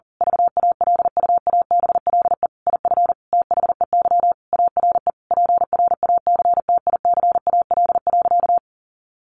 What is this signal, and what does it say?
Morse code, heard as a keyed tone, about 40 wpm, 700 Hz. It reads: VALUABLE IF THEY ARE PRACTICAL.
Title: Sentences from Top 1400 Words 40wpm